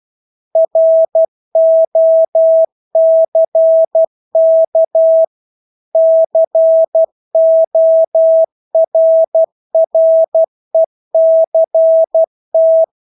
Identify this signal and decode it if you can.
Morse code, heard as a keyed tone, about 12 wpm, 650 Hz. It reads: ROCK CORRECT
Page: Engelske ord 12 wpm